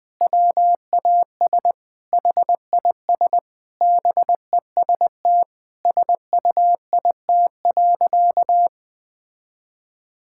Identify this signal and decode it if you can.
Morse code, heard as a keyed tone, about 20 wpm, 700 Hz. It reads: WAS HIS BEST SUIT.
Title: Sentences from Top 1600 Words 20wpm